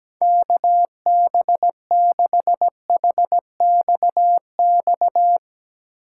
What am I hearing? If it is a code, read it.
Morse code, heard as a keyed tone, about 17 wpm, 700 Hz. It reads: KB6HXX
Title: USA Call Signs - 6 Characters - 3x Character Spacing - 17wpm